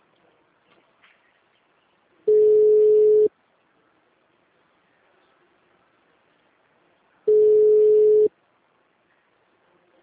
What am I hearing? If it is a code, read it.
Ringback tone (Europe, 425 Hz).
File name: out- -2010- - - .98427